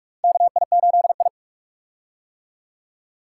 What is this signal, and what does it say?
Morse code, heard as a keyed tone, about 46 wpm, 700 Hz. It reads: KI8I
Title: K1USN SST QSO Practice - 45wpm